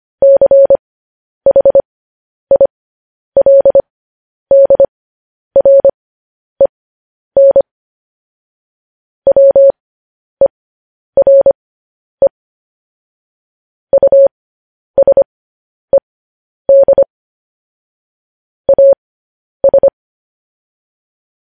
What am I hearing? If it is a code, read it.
Morse code, heard as a keyed tone, about 25 wpm, spaced as for 10 wpm, 550 Hz. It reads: CHILDREN WERE USED AS
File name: podcast / News headlines-1x-25x10